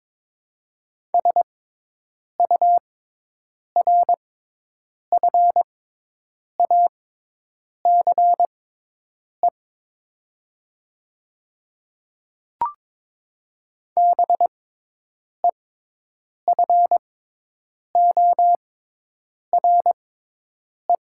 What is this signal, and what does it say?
Morse code, heard as a keyed tone, about 22 wpm, spaced as for 8 wpm, 700 Hz. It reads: SURFACE BEFORE
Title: 6x Character Spacing - Top 500 Words - 22wpm